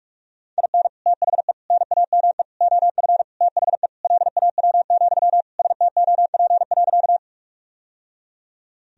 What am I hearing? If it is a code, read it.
Morse code, heard as a keyed tone, about 43 wpm, 700 Hz. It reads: IN THE NAME OF THE LAW, STOP.
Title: Sentences from Top 1000 Words - Review Part 3 - 45wpm